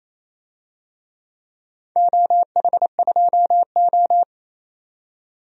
Morse code, heard as a keyed tone, about 28 wpm, 690 Hz. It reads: OH2O
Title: DXCC-100 Call Signs - 4 Characters - 28wpm